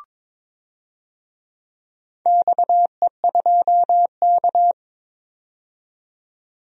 Morse code, heard as a keyed tone, about 22 wpm, 700 Hz. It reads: XE2K
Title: DXCC-100 Call Signs - 4 Characters - 22wpm